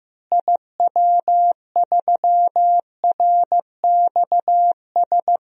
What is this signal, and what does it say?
Morse code, heard as a keyed tone, about 15 wpm, 700 Hz. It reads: IW3RXS
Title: DXCC-100 Call Signs - 6 Characters - Repeated 3x - 15wpm